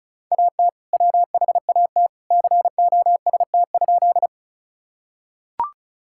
Morse code, heard as a keyed tone, about 36 wpm, 700 Hz. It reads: AT WHAT COST?
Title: Sentences from Top 1000 Words - Review Part 5 - 35wpm